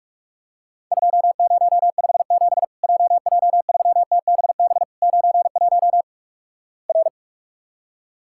Morse code, heard as a keyed tone, about 44 wpm, 700 Hz. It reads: 2057 JJ3TBB 91 R
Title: Worked All Europe DX Contest - QTC Exchange - 45wpm